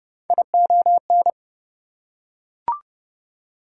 Morse code, heard as a keyed tone, about 29 wpm, 700 Hz. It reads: ION
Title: Prefixes and Suffixes 30wpm